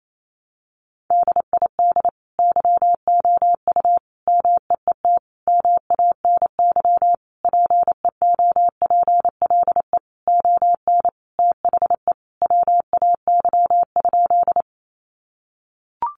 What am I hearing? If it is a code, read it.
Morse code, heard as a keyed tone, about 28 wpm, 700 Hz. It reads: DID YOU MEET MANY PEOPLE ON THE WAY?
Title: Sentences from Top 1100 Words 28wpm